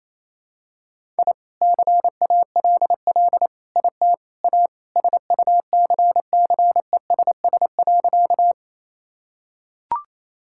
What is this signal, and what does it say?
Morse code, heard as a keyed tone, about 28 wpm, 700 Hz. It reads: I CALL IT A SUCCESS.
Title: Sentences from Top 1000 Words - Review Part 6 - 28wpm